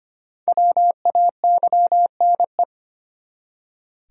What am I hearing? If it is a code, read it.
Morse code, heard as a keyed tone, about 25 wpm, 700 Hz. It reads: WAYNE
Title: USA Common Names - 25wpm